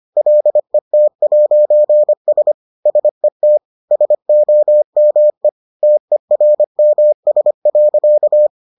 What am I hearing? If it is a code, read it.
Morse code, heard as a keyed tone, about 25 wpm, 600 Hz. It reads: LET'S SET SOME TERMS.